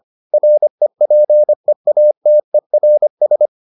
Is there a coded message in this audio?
Yes: Morse code from 0.3 to 3.5 s, heard as a keyed tone, about 25 wpm, 600 Hz. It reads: REPEATERS